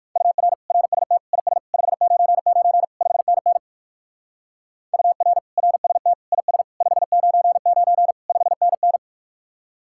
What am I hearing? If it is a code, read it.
Morse code, heard as a keyed tone, about 55 wpm, 700 Hz. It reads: UR RST IS 599 5NN UR RST IS 599 5NN
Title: Speed-Racing Common QSO Phrases - 45wpm